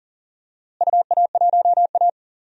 Morse code, heard as a keyed tone, about 40 wpm, 700 Hz. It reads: UA1A
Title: DXCC-100 Call Signs - 4 Characters - Repeated 3x - 40wpm